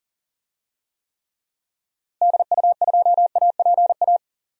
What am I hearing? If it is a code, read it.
Morse code, heard as a keyed tone, about 40 wpm, 700 Hz. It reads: DU2APA
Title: DXCC-100 Call Signs - 6 Characters - Repeated 3x - 40wpm